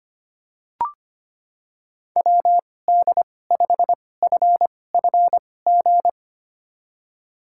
Morse code, heard as a keyed tone, about 25 wpm, spaced as for 18 wpm, 700 Hz. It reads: WD5FFG
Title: USA Call Signs - 6 Characters - 2x Character Spacing - 25wpm